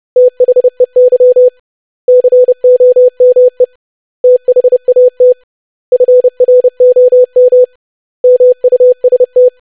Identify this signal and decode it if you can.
Morse code, heard as a keyed tone, about 30 wpm, 500 Hz. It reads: THEY COME THAT FROM MUST
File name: lesson / web / basic100-25wpm